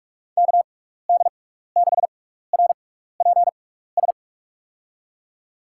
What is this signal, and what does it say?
Morse code, heard as a keyed tone, about 46 wpm, spaced as for 16 wpm, 700 Hz. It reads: KD6RPS
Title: USA Call Signs - 6 Characters - 6x Character Spacing - 45wpm